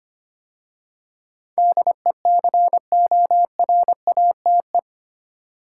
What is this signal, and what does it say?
Morse code, heard as a keyed tone, about 25 wpm, 700 Hz. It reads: DECORATE
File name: 8 Letter Words - First 3 Letters Given - 25wpm